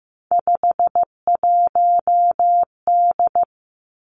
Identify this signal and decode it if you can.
Morse code, heard as a keyed tone, about 15 wpm, 700 Hz. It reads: 51D
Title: Three Challenging Letter-Number Combinations 15wpm